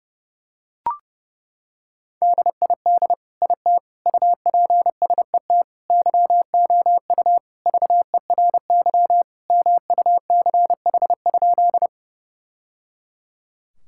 Morse code, heard as a keyed tone, about 30 wpm, 700 Hz. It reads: DID IT UPSET YOU VERY MUCH?
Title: Sentences from Top 1000 Words - Review Part 5 - 30wpm